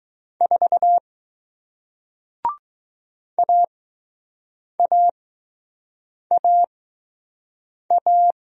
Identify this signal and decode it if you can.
Morse code, heard as a keyed tone, about 20 wpm, spaced as for 6 wpm, 700 Hz. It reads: 4 AAAA